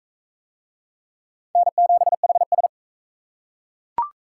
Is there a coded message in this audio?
Yes: Morse code from 1.6 to 2.7 s, heard as a keyed tone, about 38 wpm, 700 Hz. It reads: N7HS